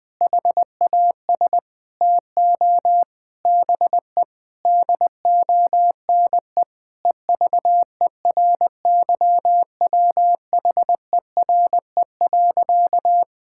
Morse code, heard as a keyed tone, about 21 wpm, 700 Hz. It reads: HAS TO BE DONE EVERYWHERE.
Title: Sentences from Top 1600 Words 20wpm